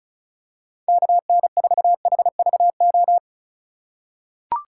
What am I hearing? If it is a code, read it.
Morse code, heard as a keyed tone, about 36 wpm, 700 Hz. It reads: KN4HVO
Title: USA Call Signs - 8x Character Spacing - 35wpm